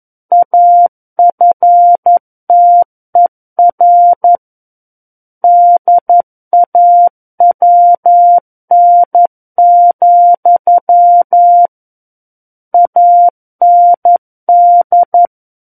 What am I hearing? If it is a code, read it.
Morse code, heard as a keyed tone, about 11 wpm, 700 Hz. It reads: AFTER DAWN, AND
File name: Book One / 02-Chapter 2 - 11 WPM